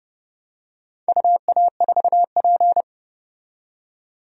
Morse code, heard as a keyed tone, about 30 wpm, 700 Hz. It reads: UA4P
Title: DXCC-100 Call Signs - 4 Characters - 30wpm